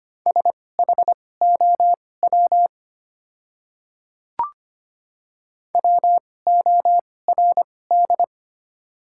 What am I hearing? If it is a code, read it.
Morse code, heard as a keyed tone, about 25 wpm, spaced as for 18 wpm, 700 Hz. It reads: SHOW WORD
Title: 2x Character Spacing - Top 200 Words - 25wpm